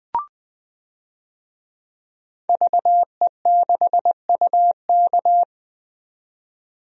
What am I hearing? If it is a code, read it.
Morse code, heard as a keyed tone, about 20 wpm, 700 Hz. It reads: VE6UK
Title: DXCC-100 Call Signs - 5 Characters - 20wpm